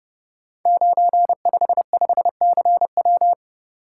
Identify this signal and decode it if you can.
Morse code, heard as a keyed tone, about 30 wpm, 700 Hz. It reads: 955CW
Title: Code Groups - 5 Letter:Numbers - 30wpm